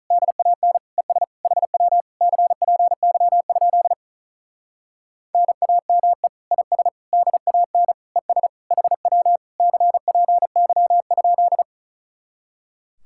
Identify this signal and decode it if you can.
Morse code, heard as a keyed tone, about 44 wpm, 700 Hz. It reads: DAN ES HW CPY? NAME IS DAN ES HW CPY?
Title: Speed-Racing Common QSO Phrases - 35wpm